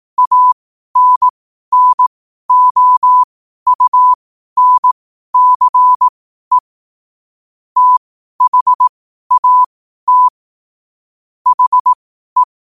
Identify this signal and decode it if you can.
Morse code, heard as a keyed tone, about 18 wpm, spaced as for 12 wpm, 1000 Hz. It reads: ANNOUNCE THAT HE
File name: qotdfiles / 12wpm